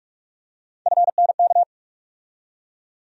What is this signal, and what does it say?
Morse code, heard as a keyed tone, about 42 wpm, 700 Hz. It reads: FNK
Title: USA Call Sign Postfixes - 3 Letters 45wpm